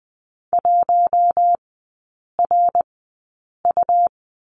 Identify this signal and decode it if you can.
Morse code, heard as a keyed tone, about 20 wpm, spaced as for 8 wpm, 700 Hz. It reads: 1RU
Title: 114 Lesson 19e - Letter w - Three Characters 20wpm